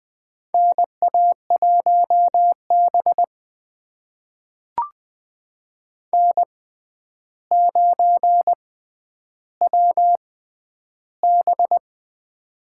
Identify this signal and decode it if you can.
Morse code, heard as a keyed tone, about 20 wpm, spaced as for 10 wpm, 700 Hz. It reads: NA1B N9WB
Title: USA Call Signs - 4 Characters - 6x Character Spacing - 20wpm